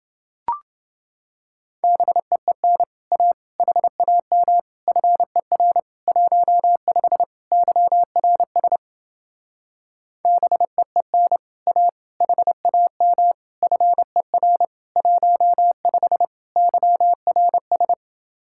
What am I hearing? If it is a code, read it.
Morse code, heard as a keyed tone, about 30 wpm, 700 Hz. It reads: BEEN A HAM FER 15 YRS BEEN A HAM FER 15 YRS